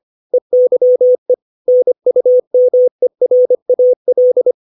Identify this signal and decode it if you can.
Morse code, heard as a keyed tone, about 25 wpm, 500 Hz. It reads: EYE NUMERAL
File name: 632 words / 632 Words 25WPM